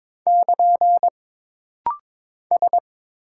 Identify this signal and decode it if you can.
Morse code, heard as a keyed tone, about 22 wpm, 700 Hz. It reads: <KN> S
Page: This practice set is composed of Letters, Numbers, common punctuations, and common prosigns.